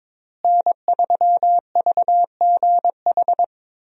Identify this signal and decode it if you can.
Morse code, heard as a keyed tone, about 22 wpm, 690 Hz. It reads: N3VGH
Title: USA Call Signs - 5 Characters - Repeated 3x - 22wpm